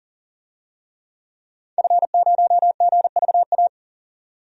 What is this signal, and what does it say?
Morse code, heard as a keyed tone, about 40 wpm, 700 Hz. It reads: F0GVA
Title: DXCC-100 Call Signs - 5 Characters - 40wpm